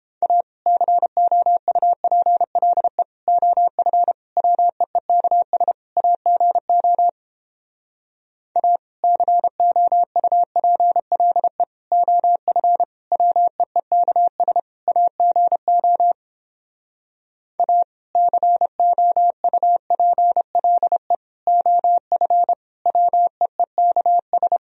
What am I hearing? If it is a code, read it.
Morse code, heard as a keyed tone, about 29 wpm, 700 Hz. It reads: A COUPLE OF WEEKS AGO A COUPLE OF WEEKS AGO A COUPLE OF WEEKS